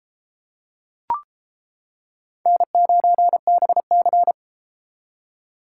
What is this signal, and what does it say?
Morse code, heard as a keyed tone, about 33 wpm, 700 Hz. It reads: N9BC